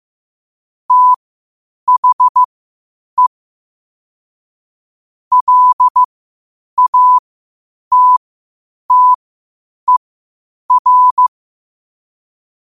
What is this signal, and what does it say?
Morse code, heard as a keyed tone, about 15 wpm, spaced as for 8 wpm, 1000 Hz. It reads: THE LATTER
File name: qotdfiles / 8wpm